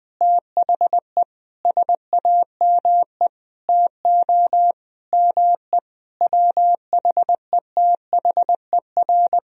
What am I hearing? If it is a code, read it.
Morse code, heard as a keyed tone, about 20 wpm, 700 Hz. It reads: THE SAME TO ME WHETHER
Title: Sentences from Top 1000 Words - Review Part 3 - 20wpm